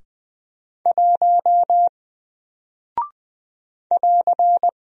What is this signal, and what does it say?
Morse code, heard as a keyed tone, about 20 wpm, 700 Hz. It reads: 1 <AR>
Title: Single Character (Letter:Number:Punctuation:Prosign) Alt 20wpm